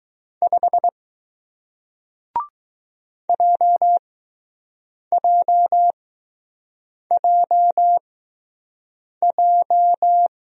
Morse code, heard as a keyed tone, about 21 wpm, spaced as for 6 wpm, 700 Hz. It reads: 5 JJJJ